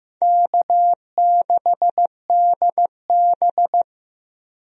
Morse code, heard as a keyed tone, about 15 wpm, 700 Hz. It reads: K6DB
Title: USA Call Signs - 4 Characters - 15wpm